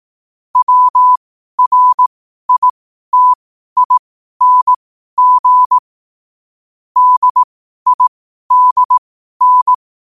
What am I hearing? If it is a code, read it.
Morse code, heard as a keyed tone, about 18 wpm, spaced as for 12 wpm, 1000 Hz. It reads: WRITING DIDN
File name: qotdfiles / 12wpm